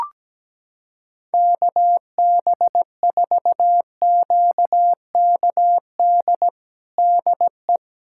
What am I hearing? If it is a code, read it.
Morse code, heard as a keyed tone, about 17 wpm, 700 Hz. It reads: KB4QKD DE
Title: USA Callsign DE Callsign - 6 Characters - 17wpm